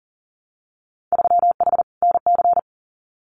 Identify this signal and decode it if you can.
Morse code, heard as a keyed tone, about 41 wpm, 700 Hz. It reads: 3H NC
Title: Winter Field Day Exchanges 40wpm